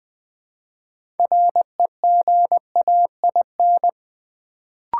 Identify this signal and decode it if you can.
Morse code, heard as a keyed tone, about 20 wpm, 700 Hz. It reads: REGAIN
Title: Build A Word - 2 Letters At A Time - 6-Letter Words - 20wpm